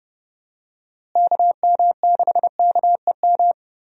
Morse code, heard as a keyed tone, about 31 wpm, 700 Hz. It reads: KM6KEM
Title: USA Call Signs - 6 Characters - Repeated 2x - 30wpm